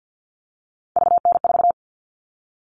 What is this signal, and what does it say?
Morse code, heard as a keyed tone, about 50 wpm, 700 Hz. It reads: VN4
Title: All Cut Numbers - 3-Digit 50wpm